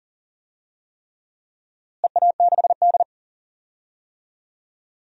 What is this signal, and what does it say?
Morse code, heard as a keyed tone, about 40 wpm, 700 Hz. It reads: EA6D
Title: DXCC-100 Call Signs - 4 Characters - Repeated 3x - 40wpm